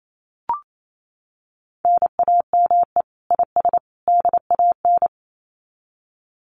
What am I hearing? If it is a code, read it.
Morse code, heard as a keyed tone, about 28 wpm, 700 Hz. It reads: NAME IS DAN
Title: Common QSO Phrases 28wpm